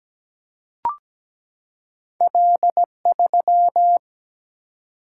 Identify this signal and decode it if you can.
Morse code, heard as a keyed tone, about 17 wpm, 700 Hz. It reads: L3
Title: Single-Letter Single-Number 17wpm